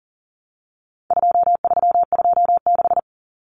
Morse code, heard as a keyed tone, about 40 wpm, 700 Hz. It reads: 2326
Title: Common Cut Numbers - 4-Digit 40wpm